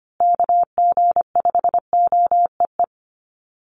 Morse code, heard as a keyed tone, about 25 wpm, 700 Hz. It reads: KG5OEE
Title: USA Call Signs Repeated 2x 25wpm